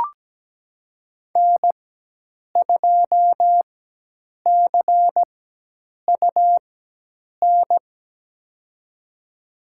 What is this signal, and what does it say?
Morse code, heard as a keyed tone, about 17 wpm, spaced as for 8 wpm, 700 Hz. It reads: N2CUN
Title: USA Call Signs - 5 Characters - 4x Character Spacing - 17wpm